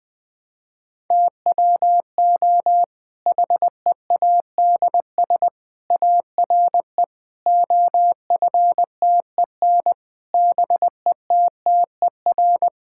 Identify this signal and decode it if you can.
Morse code, heard as a keyed tone, about 20 wpm, 700 Hz. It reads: TWO HEADS ARE OFTEN BETTER
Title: Difficult Sentences with Hints - 20wpm